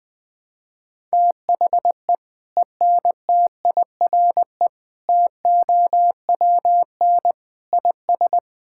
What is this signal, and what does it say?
Morse code, heard as a keyed tone, about 20 wpm, 700 Hz. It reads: THE ENTIRE TOWN IS
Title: Sentences from Top 1000 Words - Review Part 5 - 20wpm